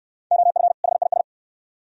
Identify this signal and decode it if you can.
Morse code, heard as a keyed tone, about 64 wpm, 700 Hz. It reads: OF HIS